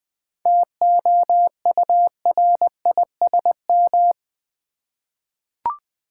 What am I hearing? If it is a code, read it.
Morse code, heard as a keyed tone, about 20 wpm, 690 Hz. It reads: TOURISM
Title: 7 Letter Words - First 3 Letters Given - 20wpm